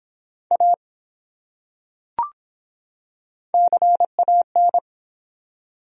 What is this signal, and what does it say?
Morse code, heard as a keyed tone, about 23 wpm, 700 Hz. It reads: A CAN